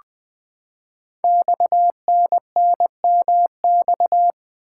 Morse code, heard as a keyed tone, about 20 wpm, 700 Hz. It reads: XNNMX
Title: Code Groups - 5 Letters - 20wpm